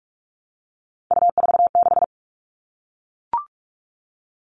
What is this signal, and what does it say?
Morse code, heard as a keyed tone, about 44 wpm, 700 Hz. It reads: U46
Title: All Cut Numbers - 3-Digit 45wpm